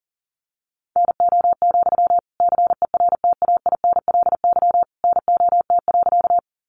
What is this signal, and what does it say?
Morse code, heard as a keyed tone, about 40 wpm, 700 Hz. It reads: NO, CERTAINLY NOT.
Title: Sentences from Top 900 Words 40wpm